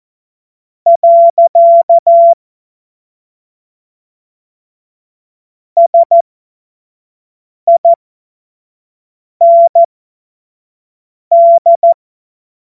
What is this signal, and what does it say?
Morse code, heard as a keyed tone, about 14 wpm, spaced as for 5 wpm, 680 Hz. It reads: . SIND
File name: Morsekurs / Witze / Witz 04. April 2023 25 F70